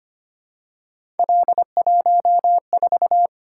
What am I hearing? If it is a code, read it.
Morse code, heard as a keyed tone, about 25 wpm, 700 Hz. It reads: L14
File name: Three Letter-Number Combinations - 25wpm